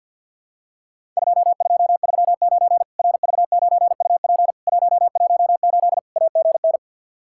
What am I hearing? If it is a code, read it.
Morse code, heard as a keyed tone, about 48 wpm, 700 Hz. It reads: 2239 RV9UP 118 AGN
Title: Worked All Europe DX Contest - QTC Exchange with Repeats - 50wpm